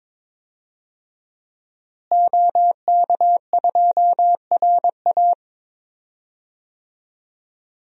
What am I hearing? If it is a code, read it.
Morse code, heard as a keyed tone, about 22 wpm, 700 Hz. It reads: OK2RA
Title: DXCC-100 Call Signs - 5 Characters - Repeated 2x - 22wpm